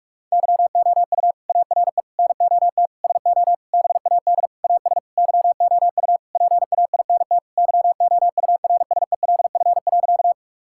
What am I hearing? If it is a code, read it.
Morse code, heard as a keyed tone, about 46 wpm, 700 Hz. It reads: YOU ARE NOT SO BAD AS YOU PAINT YOURSELF.
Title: Sentences from Top 1000 Words - Review Part 6 - 45wpm